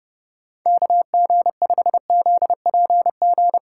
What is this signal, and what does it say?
Morse code, heard as a keyed tone, about 31 wpm, 700 Hz. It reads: KG5ZPG